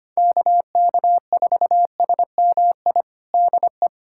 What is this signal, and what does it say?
Morse code, heard as a keyed tone, about 25 wpm, 700 Hz. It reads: KK4SMI DE
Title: USA Callsign DE Callsign - 6 Characters - 25wpm